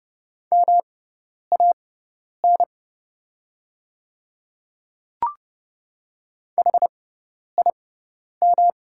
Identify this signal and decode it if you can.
Morse code, heard as a keyed tone, about 30 wpm, spaced as for 10 wpm, 700 Hz. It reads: MAN HIM
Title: 6x Character Spacing - Top 200 Words - 30wpm